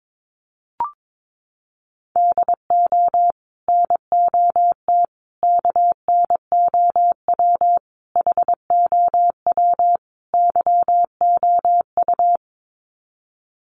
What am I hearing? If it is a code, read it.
Morse code, heard as a keyed tone, about 22 wpm, 700 Hz. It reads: DO NOT KNOW HOW YOU
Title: Sets of 5 Words Encore 22wpm